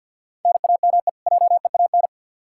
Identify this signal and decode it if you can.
Morse code, heard as a keyed tone, about 52 wpm, 700 Hz. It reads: NAME JEAN
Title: Practice QSO - Exchange 1 - Beginning - 50wpm